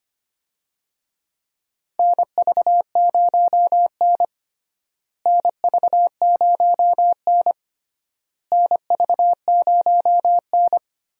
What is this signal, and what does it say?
Morse code, heard as a keyed tone, about 25 wpm, 700 Hz. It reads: NV0N NV0N NV0N
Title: USA Call Signs - 4 Characters - Repeated 3x - 25wpm